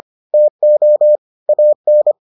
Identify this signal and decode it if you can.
Morse code, heard as a keyed tone, about 25 wpm, 600 Hz. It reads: TO AN